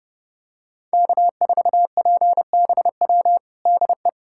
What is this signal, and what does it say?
Morse code, heard as a keyed tone, about 32 wpm, 700 Hz. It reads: K4PBW DE
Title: USA Callsign DE Callsign - 5 Characters - 30wpm